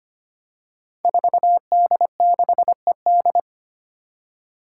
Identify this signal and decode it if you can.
Morse code, heard as a keyed tone, about 25 wpm, 700 Hz. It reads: 4D6ED
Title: Five Challenging Letter-Number Combinations 25wpm